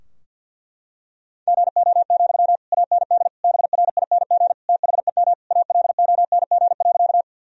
Morse code, heard as a keyed tone, about 49 wpm, 700 Hz. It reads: GO, AND BRING THEM ALONG.
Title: Sentences from Top 1000 Words - Review Part 3 - 50wpm